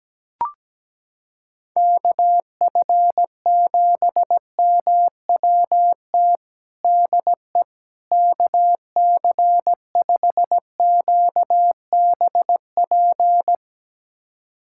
Morse code, heard as a keyed tone, about 17 wpm, 700 Hz. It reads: KF7MWT DE KC5QBP
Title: USA Callsign DE Callsign - 6 Characters - 17wpm